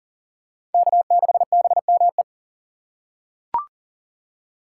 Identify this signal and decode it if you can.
Morse code, heard as a keyed tone, about 40 wpm, 700 Hz. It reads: K6BME
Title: USA Call Signs - 5 Characters - 4x Character Spacing - 40wpm